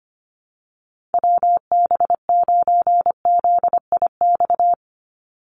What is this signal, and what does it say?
Morse code, heard as a keyed tone, about 25 wpm, 700 Hz. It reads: WB9ZIX
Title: USA Call Signs 25wpm